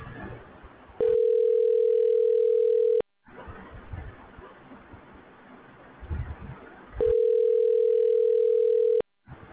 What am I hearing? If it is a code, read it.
Ringback tone (North America), 440 + 480 Hz.
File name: recordings / RE409fc507f5e9dec88698da124b86f3fa